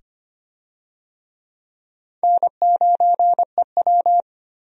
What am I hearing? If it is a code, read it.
Morse code, heard as a keyed tone, about 25 wpm, 700 Hz. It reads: N9EW
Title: USA Call Signs - 4 Characters - 25wpm